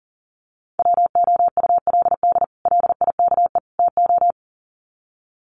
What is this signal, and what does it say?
Morse code, heard as a keyed tone, about 40 wpm, 700 Hz. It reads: WOULD LIKE TO
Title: Sets of 3 Words 40wpm